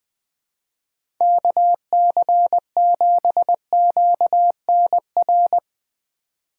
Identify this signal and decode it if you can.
Morse code, heard as a keyed tone, about 20 wpm, 700 Hz. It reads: KC7QNR
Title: USA Call Signs - 6 Characters - 4x Character Spacing - 20wpm